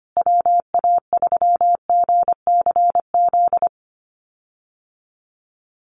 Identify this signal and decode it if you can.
Morse code, heard as a keyed tone, about 25 wpm, 700 Hz. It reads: WA3GCZ
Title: USA Call Signs Repeated 3x 25wpm